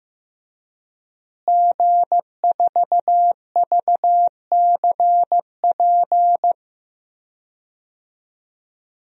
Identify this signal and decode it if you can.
Morse code, heard as a keyed tone, about 15 wpm, 700 Hz. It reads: G4VCP
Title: DXCC-100 Call Signs - 5 Characters - Repeated 2x - 15wpm